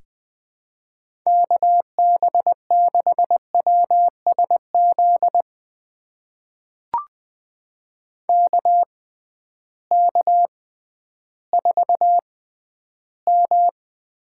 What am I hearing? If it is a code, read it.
Morse code, heard as a keyed tone, about 20 wpm, spaced as for 12 wpm, 700 Hz. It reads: KB6WSZ KK4M
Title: USA Call Signs - 6 Characters - 6x Character Spacing - 20wpm